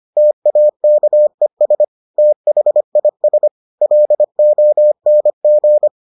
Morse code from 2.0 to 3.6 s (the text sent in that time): THIS